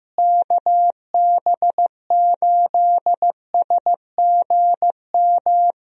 Morse code, heard as a keyed tone, about 15 wpm, 700 Hz. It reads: KB8SGM
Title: USA Call Signs - 6 Characters - Repeated 3x - 15wpm